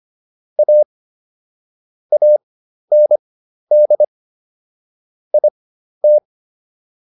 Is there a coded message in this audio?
Yes: Morse code from 0.6 to 6.2 s, heard as a keyed tone, about 25 wpm, spaced as for 12 wpm, 600 Hz. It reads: A AND IT